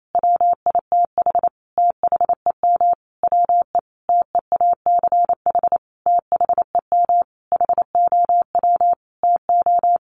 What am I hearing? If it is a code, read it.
Morse code, heard as a keyed tone, about 28 wpm, 700 Hz. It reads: WITH THEM WE TEACH THEM HOW TO
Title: Sentences from Top 1300 Words 28wpm